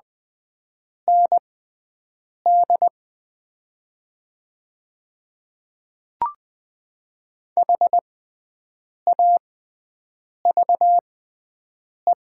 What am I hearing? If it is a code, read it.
Morse code, heard as a keyed tone, about 20 wpm, spaced as for 7 wpm, 700 Hz. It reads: ND HAVE
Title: 6x Character Spacing - Top 200 Words - 20wpm